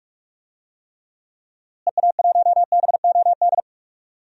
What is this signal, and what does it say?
Morse code, heard as a keyed tone, about 46 wpm, 700 Hz. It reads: EA1BOD
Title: DXCC-100 Call Signs - 6 Characters - Repeated 2x - 45wpm